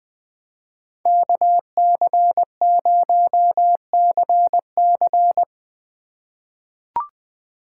Morse code, heard as a keyed tone, about 20 wpm, 700 Hz. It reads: KC0CC
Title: USA Call Signs - 3x Character Spacing - 20wpm